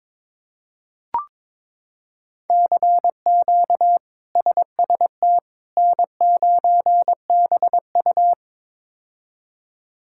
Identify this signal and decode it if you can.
Morse code, heard as a keyed tone, about 22 wpm, 700 Hz. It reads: CQ SST N9BU
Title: K1USN SST QSO Practice - 22wpm